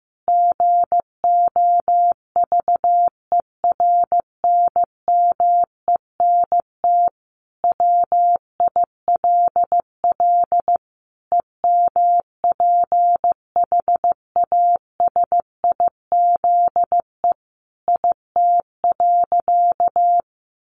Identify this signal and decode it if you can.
Morse code, heard as a keyed tone, about 15 wpm, 700 Hz. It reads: GOVERNMENT WILL EMPHASIZE IT.
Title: Sentences from Top 1500 Words 15wpm